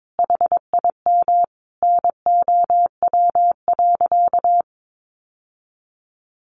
Morse code, heard as a keyed tone, about 22 wpm, 700 Hz. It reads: HIM NOW.
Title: Sentences from Top 1100 Words 22wpm